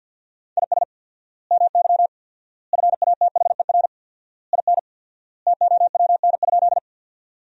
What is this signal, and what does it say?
Morse code, heard as a keyed tone, about 53 wpm, 700 Hz. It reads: IS MY FATHER IN TOWN?
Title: 4x Word-Spacing - Sentences from Top 500 Words - 50wpm